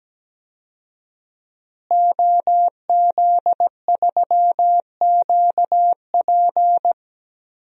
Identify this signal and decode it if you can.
Morse code, heard as a keyed tone, about 17 wpm, 700 Hz. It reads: OZ3QP
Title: DXCC-100 Call Signs - 5 Characters - Repeated 3x - 17wpm